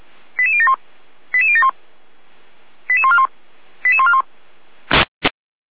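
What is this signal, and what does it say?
Five-tone selective call (ZVEI), heard as quick runs of five tones, tones of about 70 ms. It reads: 88071 88071 88141 88141
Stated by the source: ZVEI five-tone selective call received by SDR.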